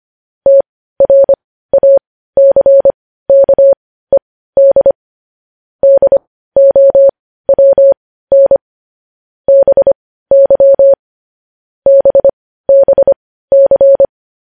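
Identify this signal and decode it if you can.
Morse code, heard as a keyed tone, about 25 wpm, spaced as for 15 wpm, 550 Hz. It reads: TRACKED DOWN BY BBC
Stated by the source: Morse code sending of news headlines at 15 wpm.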